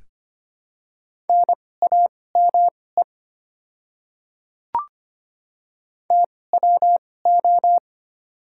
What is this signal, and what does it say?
Morse code, heard as a keyed tone, about 25 wpm, spaced as for 18 wpm, 700 Hz. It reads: NAME TWO
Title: 2x Character Spacing - Top 200 Words - 25wpm